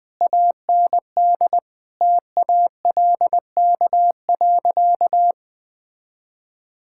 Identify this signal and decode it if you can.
Morse code, heard as a keyed tone, about 21 wpm, 700 Hz. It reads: AND TALK.
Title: Sentences from Top 1000 Words - Review Part 6 - 20wpm